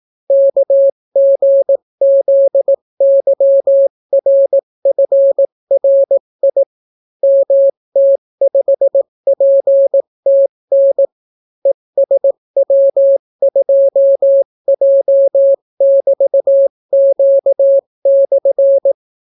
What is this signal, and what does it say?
Morse code, heard as a keyed tone, about 18 wpm, 550 Hz. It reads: KGZYRFRI MT5PTN ESW2J<BT>Q/